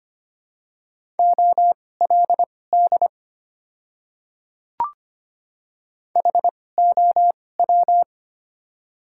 Morse code, heard as a keyed tone, about 25 wpm, spaced as for 18 wpm, 700 Hz. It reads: OLD HOW
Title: 2x Character Spacing - Top 300 Words - 25wpm